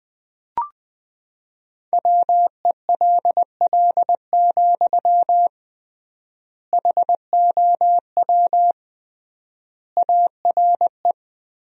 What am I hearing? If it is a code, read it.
Morse code, heard as a keyed tone, about 21 wpm, 700 Hz. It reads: WELL, HOW ARE
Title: 3x Word-Spacing - Sentences from Top 200 Words - 20wpm